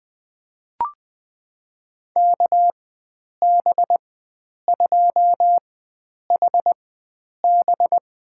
Morse code, heard as a keyed tone, about 21 wpm, spaced as for 9 wpm, 700 Hz. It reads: KB2HB
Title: USA Call Signs - 4x Character Spacing - 20wpm